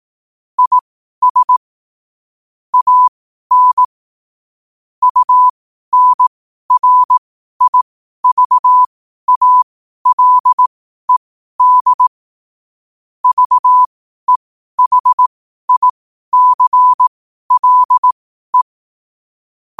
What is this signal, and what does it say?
Morse code, heard as a keyed tone, about 18 wpm, spaced as for 12 wpm, 1000 Hz. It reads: IS AN UNRIVALED VEHICLE
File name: qotdfiles / 12wpm